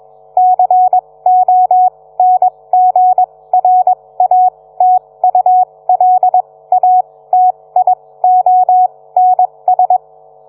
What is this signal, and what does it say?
Morse code, heard as a keyed tone, about 21 wpm, spaced as for 17 wpm, 740 Hz. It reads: CONGRATULATIONS